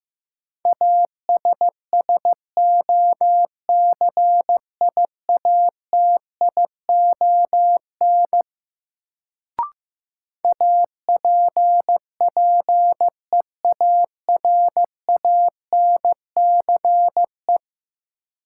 Morse code, heard as a keyed tone, about 15 wpm, 700 Hz. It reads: ASSOCIATION APPEARANCE
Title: Sentences from Top 1100 Words - New Word Drill - 15wpm